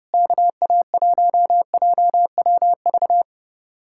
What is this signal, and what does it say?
Morse code, heard as a keyed tone, about 36 wpm, 700 Hz. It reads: KA1JWV
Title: Speed-Racing Common QSO Phrases Encore - 30wpm